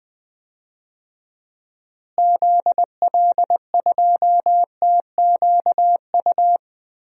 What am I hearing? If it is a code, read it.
Morse code, heard as a keyed tone, about 20 wpm, 700 Hz. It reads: ZL2TQU
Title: DXCC-100 Call Signs - 6 Characters - Repeated 2x - 20wpm